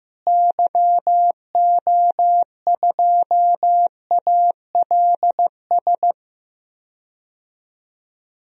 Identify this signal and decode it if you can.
Morse code, heard as a keyed tone, about 15 wpm, 700 Hz. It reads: YO2ALS
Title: DXCC-100 Call Signs - 6 Characters - Repeated 3x - 15wpm